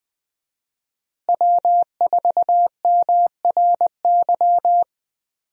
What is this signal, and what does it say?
Morse code, heard as a keyed tone, about 20 wpm, 700 Hz. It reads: W4MRY
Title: USA Call Signs - 5 Characters - Repeated 2x - 20wpm